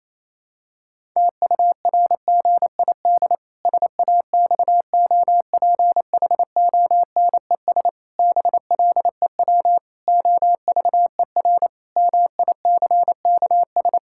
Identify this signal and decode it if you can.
Morse code, heard as a keyed tone, about 28 wpm, 700 Hz. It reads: TURGID SAXOPHONES BLEW OVER MICKS
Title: Pangram Challenge v2 - 28wpm